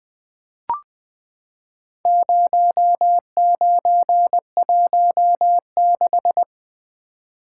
Morse code, heard as a keyed tone, about 20 wpm, 700 Hz. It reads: 0916
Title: Four Letter-Number Combinations - 20wpm